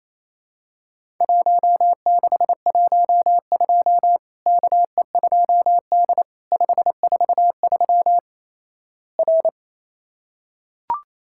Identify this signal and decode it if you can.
Morse code, heard as a keyed tone, about 28 wpm, 700 Hz. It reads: 1612 KE2D 543 R
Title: Worked All Europe DX Contest - QTC Exchange - 28wpm